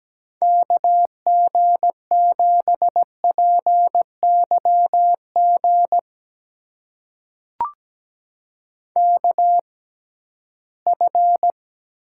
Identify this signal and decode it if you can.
Morse code, heard as a keyed tone, about 17 wpm, spaced as for 13 wpm, 700 Hz. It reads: KG7PYG KF
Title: USA Call Signs - 6 Characters - 6x Character Spacing - 17wpm